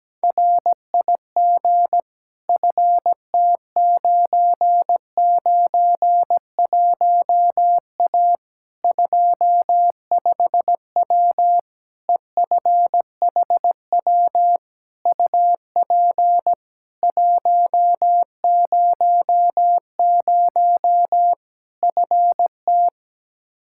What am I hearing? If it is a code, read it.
Morse code, heard as a keyed tone, about 17 wpm, 700 Hz. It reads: RIG FT991A 25W EFHW UP 100 FT
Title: Practice QSO - Exchange 2 - Info - 17wpm